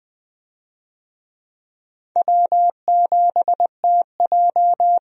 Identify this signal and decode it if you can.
Morse code, heard as a keyed tone, about 20 wpm, 700 Hz. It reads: W7TJ
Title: USA Call Signs - 4 Characters - 20wpm